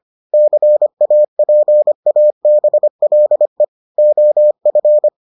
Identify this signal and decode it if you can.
Morse code, heard as a keyed tone, about 25 wpm, 600 Hz. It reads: CAPABLE OF